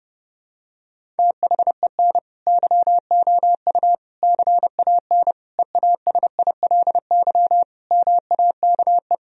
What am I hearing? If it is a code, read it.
Morse code, heard as a keyed tone, about 30 wpm, 700 Hz. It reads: THEN YOU CAN EASILY MAKE
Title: Sentences from Top 1000 Words - Review Part 1 - 30wpm